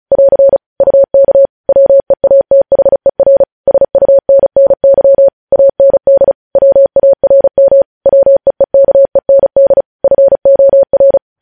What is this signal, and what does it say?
Morse code, heard as a keyed tone, about 36 wpm, 550 Hz. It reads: <AR> UK WEATHER SUNNY AND WARM WEEKEND FOR
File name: podcast / News headlines-1x-35x35